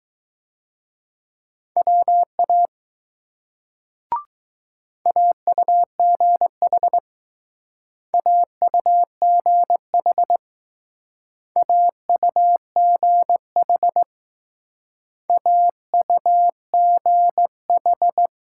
Morse code, heard as a keyed tone, about 20 wpm, 700 Hz. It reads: WA AUGH AUGH AUGH AUGH